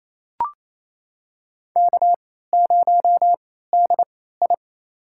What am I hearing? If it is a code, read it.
Morse code, heard as a keyed tone, about 27 wpm, spaced as for 16 wpm, 700 Hz. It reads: K0DI
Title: USA Call Signs - 3x Character Spacing - 28wpm